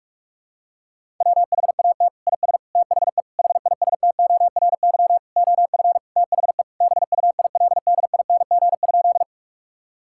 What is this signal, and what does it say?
Morse code, heard as a keyed tone, about 46 wpm, 700 Hz. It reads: WHAT IS THE HISTORY OF THE BUILDING?
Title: Sentences from Top 1000 Words - Review Part 5 - 45wpm